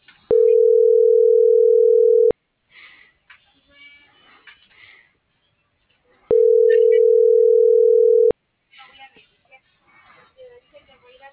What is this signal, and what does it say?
Ringback tone (North America), 440 + 480 Hz.